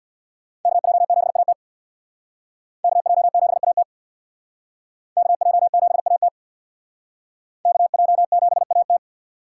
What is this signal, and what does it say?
Morse code, heard as a keyed tone, about 65 wpm, 700 Hz. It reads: KJ7AT KJ7AT KJ7AT KJ7AT